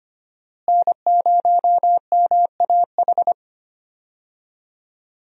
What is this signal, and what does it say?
Morse code, heard as a keyed tone, about 25 wpm, 700 Hz. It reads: N0MAH
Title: USA Call Signs - 5 Characters - Repeated 3x - 25wpm